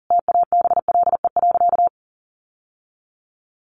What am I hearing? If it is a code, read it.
Morse code, heard as a keyed tone, about 40 wpm, 700 Hz. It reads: TABLE.
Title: Sentences from Top 1300 Words 40wpm